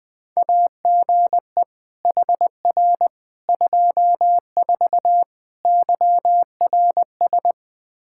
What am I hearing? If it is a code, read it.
Morse code, heard as a keyed tone, about 21 wpm, 700 Hz. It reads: AGE HR 24 YRS
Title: K1USN SST QSO Practice 20wpm